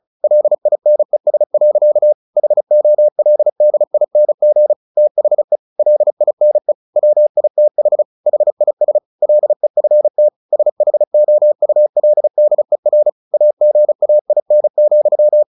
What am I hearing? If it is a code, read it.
Morse code, heard as a keyed tone, about 34 wpm, 600 Hz. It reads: LINES. HOLDING THE LINE WITH HIS LEFT SHOULDER AGAIN,
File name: the old man and the sea / the old man and the sea 1 35wpm